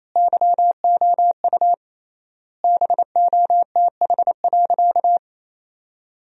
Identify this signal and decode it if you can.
Morse code, heard as a keyed tone, about 27 wpm, 700 Hz. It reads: YOU BOTH.
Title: 3x Word-Spacing - Sentences from Top 300 Words - 28wpm